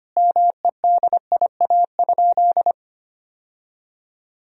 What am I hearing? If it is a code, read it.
Morse code, heard as a keyed tone, about 25 wpm, 700 Hz. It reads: MEDIA?
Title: Sentences from Top 1000 Words - Review Part 4 - 25wpm